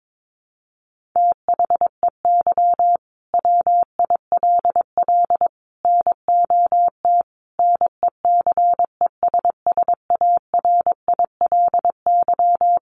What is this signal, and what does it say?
Morse code, heard as a keyed tone, about 22 wpm, 700 Hz. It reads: THEY WILL NOT NECESSARILY
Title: Sentences from Top 1400 Words 22wpm